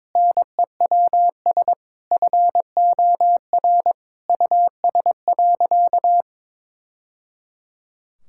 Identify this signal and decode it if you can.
Morse code, heard as a keyed tone, about 22 wpm, 700 Hz. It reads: NEWS FOR US.
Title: Sentences from Top 1000 Words - Review Part 5 - 22wpm